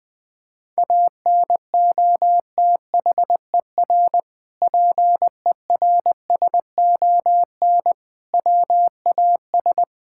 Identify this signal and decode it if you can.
Morse code, heard as a keyed tone, about 20 wpm, 700 Hz. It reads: ANOTHER PERSON WAS
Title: Sentences from Top 1000 Words - Review Part 5 - 20wpm